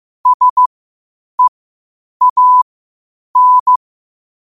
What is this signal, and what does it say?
Morse code, heard as a keyed tone, about 15 wpm, spaced as for 8 wpm, 1000 Hz. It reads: SEAN
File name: 8wpm